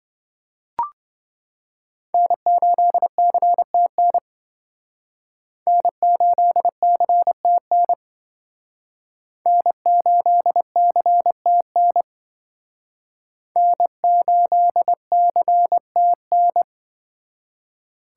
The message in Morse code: N8CTN N8CTN N8CTN N8CTN

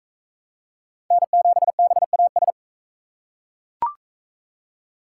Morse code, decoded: N7BAS

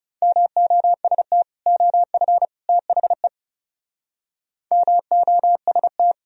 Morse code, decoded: MOST OF THE MOST

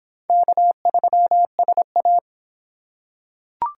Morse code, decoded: K3SA

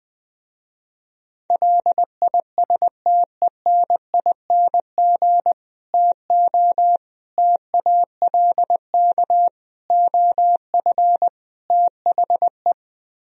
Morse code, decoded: LISTENING TO TALK OF THE

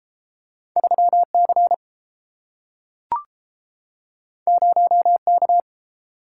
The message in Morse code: 3C 0K

